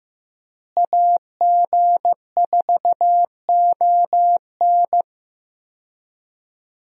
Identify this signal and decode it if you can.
Morse code, heard as a keyed tone, about 15 wpm, 700 Hz. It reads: AG4ON
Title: USA Call Signs - 5 Characters - Repeated 2x - 15wpm